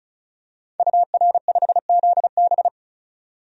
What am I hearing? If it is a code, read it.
Morse code, heard as a keyed tone, about 36 wpm, 700 Hz. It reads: UR5ZB
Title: DXCC-100 Call Signs - 5 Characters - 35wpm